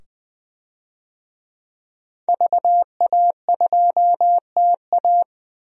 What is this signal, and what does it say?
Morse code, heard as a keyed tone, about 20 wpm, 700 Hz. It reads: VA2TA
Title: DXCC-100 Call Signs - 5 Characters - 20wpm